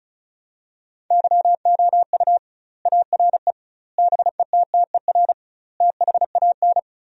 Morse code, heard as a keyed tone, about 36 wpm, 700 Hz. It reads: YOU ARE BETTER THAN
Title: 2x Word-Spacing - Sentences from Top 400 Words - 35wpm